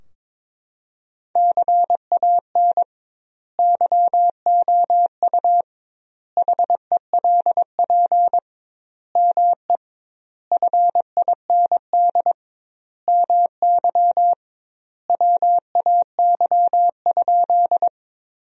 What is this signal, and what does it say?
Morse code, heard as a keyed tone, about 22 wpm, 700 Hz. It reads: CAN YOU HELP ME FIND MY WAY?
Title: 2x Word-Spacing - Sentences from Top 200 Words - 22wpm